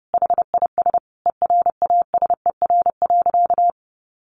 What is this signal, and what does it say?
Morse code, heard as a keyed tone, about 30 wpm, 700 Hz. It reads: HIS ERASER.
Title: Additional Easy Phrases 30wpm